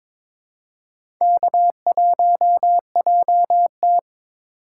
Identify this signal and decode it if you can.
Morse code, heard as a keyed tone, about 22 wpm, 700 Hz. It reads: K1JT
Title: USA Call Signs - 4 Characters - 3x Character Spacing - 22wpm